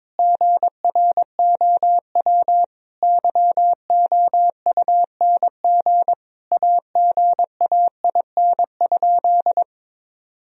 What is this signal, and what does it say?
Morse code, heard as a keyed tone, about 22 wpm, 700 Hz. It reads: GROW YOUNG AGAIN?
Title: Sentences from Top 1000 Words - Review Part 1 - 22wpm